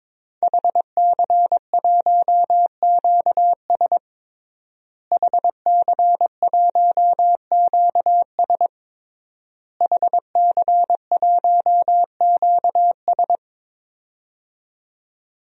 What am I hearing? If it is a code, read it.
Morse code, heard as a keyed tone, about 22 wpm, 700 Hz. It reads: HC1QS HC1QS HC1QS
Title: DXCC-100 Call Signs - 5 Characters - Repeated 3x - 22wpm